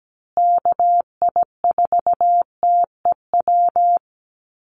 Morse code, heard as a keyed tone, about 17 wpm, 700 Hz. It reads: KI4TEW
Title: USA Call Signs Repeated 3x 17wpm